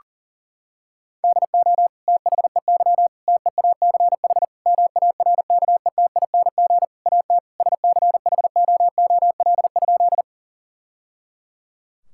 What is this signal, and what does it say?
Morse code, heard as a keyed tone, about 40 wpm, 700 Hz. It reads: DO THEY TEACH MARKETING AT SCHOOL?
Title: Sentences from Top 1000 Words - Review Part 5 - 40wpm